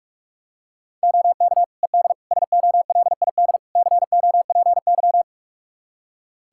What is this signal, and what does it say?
Morse code, heard as a keyed tone, about 41 wpm, 700 Hz. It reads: OK ED SOLID COPY
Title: Practice QSO - Exchange 3 - More Info - 45wpm